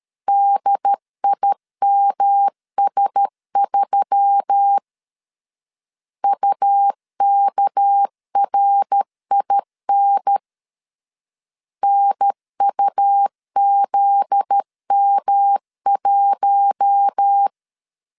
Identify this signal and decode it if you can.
Morse code, heard as a keyed tone, about 13 wpm, 800 Hz. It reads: DIMS3 UKRIN NUZM1
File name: LEZIONI / 8 YJ45 / 8 YJ4 12wpm 2151